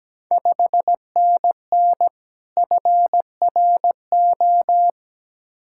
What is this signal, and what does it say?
Morse code, heard as a keyed tone, about 17 wpm, 700 Hz. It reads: 5NN FRO
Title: Ontario QSO Party Exchange 17wpm